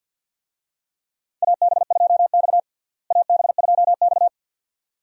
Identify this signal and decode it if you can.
Morse code, heard as a keyed tone, about 46 wpm, 700 Hz. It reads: AB2X AB2X
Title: USA Call Signs - 4 Characters - Repeated 2x - 50wpm